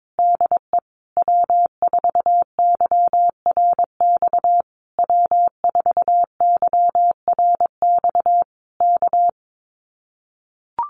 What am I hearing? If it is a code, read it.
Morse code, heard as a keyed tone, about 22 wpm, 700 Hz. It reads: DE W4YRX W4YRX K
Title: Common QSO Phrases Encore 22wpm